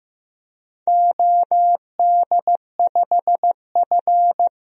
Morse code, heard as a keyed tone, about 15 wpm, 700 Hz. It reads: OD5F
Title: DXCC-100 Call Signs - 4 Characters - Repeated 3x - 15wpm